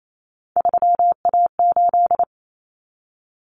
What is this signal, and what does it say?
Morse code, heard as a keyed tone, about 28 wpm, 700 Hz. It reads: 3A8
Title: Common Cut Numbers - 3-Digit 28wpm